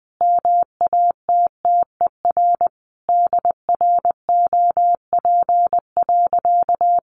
Morse code, heard as a keyed tone, about 20 wpm, 700 Hz. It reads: MATTER DROP.
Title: Sentences from Top 1400 Words 20wpm